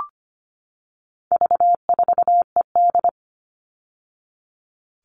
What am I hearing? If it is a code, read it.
Morse code, heard as a keyed tone, about 25 wpm, 700 Hz. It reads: V4ED
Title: All Cut Numbers - 4-Digit 25wpm